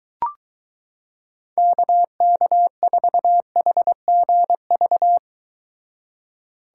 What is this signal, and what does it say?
Morse code, heard as a keyed tone, about 23 wpm, 700 Hz. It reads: KK4HGV